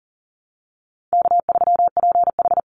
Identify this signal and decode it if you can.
Morse code, heard as a keyed tone, about 40 wpm, 700 Hz. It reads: K3PH
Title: Call Signs Repeated 3x - 40wpm